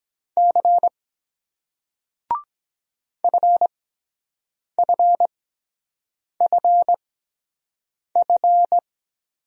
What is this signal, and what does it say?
Morse code, heard as a keyed tone, about 23 wpm, spaced as for 7 wpm, 700 Hz. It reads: C FFFF